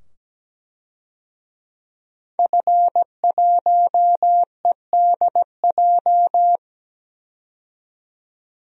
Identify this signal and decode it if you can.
Morse code, heard as a keyed tone, about 17 wpm, 700 Hz. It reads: F1EDJ
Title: DXCC-100 Call Signs - 5 Characters - 17wpm